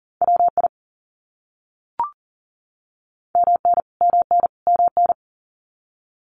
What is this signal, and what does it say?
Morse code, heard as a keyed tone, about 40 wpm, 700 Hz. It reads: WI MN MN MN
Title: US States Repeated 3x 40wpm